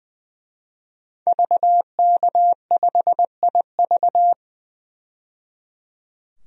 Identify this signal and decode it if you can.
Morse code, heard as a keyed tone, about 20 wpm, 700 Hz. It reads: VK5IV
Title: DXCC-100 Call Signs - 5 Characters - 20wpm